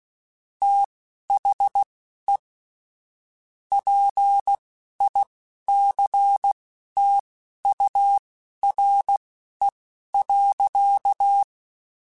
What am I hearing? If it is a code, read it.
Morse code, heard as a keyed tone, about 16 wpm, spaced as for 11 wpm, 800 Hz. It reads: THE PICTURE.
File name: Technical Subjects / Technical Subjects-mp3s / AND Gate @10